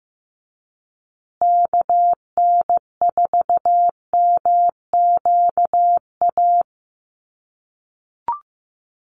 Morse code, heard as a keyed tone, about 15 wpm, 700 Hz. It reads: KN4MQA
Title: USA Call Signs Repeated 3x 15wpm